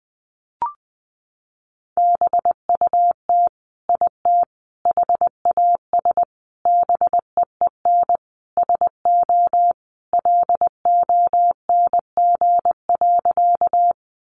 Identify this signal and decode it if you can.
Morse code, heard as a keyed tone, about 20 wpm, 700 Hz. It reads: BUT IT HAS BEEN SO LONG.
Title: Sentences from Top 100 Words 20wpm